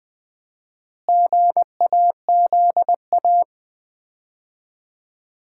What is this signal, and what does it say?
Morse code, heard as a keyed tone, about 20 wpm, 700 Hz. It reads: GAZA
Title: 4x Word-Spacing - Sets of 5 Words - 20wpm